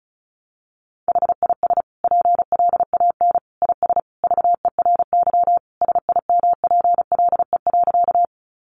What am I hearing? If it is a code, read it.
Morse code, heard as a keyed tone, about 34 wpm, 700 Hz. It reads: HIS PLAN IS VERY SIMPLE.
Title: Sentences from Top 500 Words 35wpm